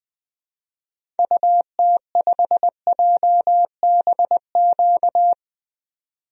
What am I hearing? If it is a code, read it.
Morse code, heard as a keyed tone, about 20 wpm, 700 Hz. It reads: UT5JBQ
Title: DXCC-100 Call Signs - 6 Characters - 20wpm